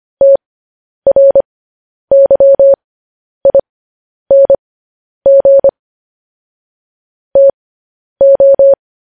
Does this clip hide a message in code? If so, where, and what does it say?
Yes: Morse code from 0.2 to 8.7 s, heard as a keyed tone, about 25 wpm, spaced as for 10 wpm, 550 Hz. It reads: TRYING TO